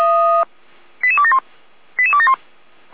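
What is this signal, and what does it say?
Five-tone selective call (ZVEI), heard as quick runs of five tones, tones of about 70 ms. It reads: 88271 88271
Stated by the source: ZVEI five-tone call 88533 received on SDR, from the feuerwehr-orlamuende.org server, a fire brigade.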